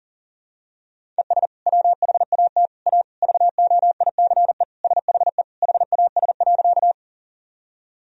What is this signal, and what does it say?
Morse code, heard as a keyed tone, about 37 wpm, 700 Hz. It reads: ES WHAT A VOICE SHE HAS.
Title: Sentences from Top 500 Words and Abbreviations 40wpm